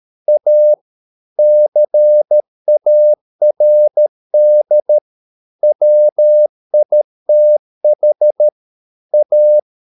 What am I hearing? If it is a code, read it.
Morse code, heard as a keyed tone, about 13 wpm, 600 Hz. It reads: A CARD WITH A